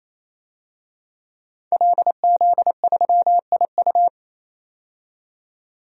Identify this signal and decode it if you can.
Morse code, heard as a keyed tone, about 28 wpm, 700 Hz. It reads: LZ3IU
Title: DXCC-100 Call Signs - 5 Characters - 28wpm